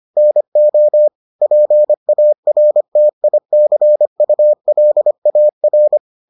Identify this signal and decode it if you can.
Morse code, heard as a keyed tone, about 25 wpm, 600 Hz. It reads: NO PARTICULAR